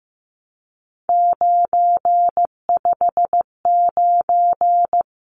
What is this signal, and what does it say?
Morse code, heard as a keyed tone, about 15 wpm, 700 Hz. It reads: 959
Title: Numbers - 3-Digit 15wpm